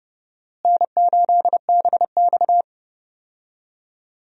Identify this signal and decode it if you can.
Morse code, heard as a keyed tone, about 32 wpm, 700 Hz. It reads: N8BX
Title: USA Call Signs - 4 Characters - Repeated 3x - 30wpm